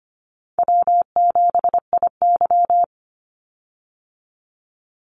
Morse code, heard as a keyed tone, about 25 wpm, 700 Hz. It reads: W7IY
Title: Call Signs Repeated 2x - 25wpm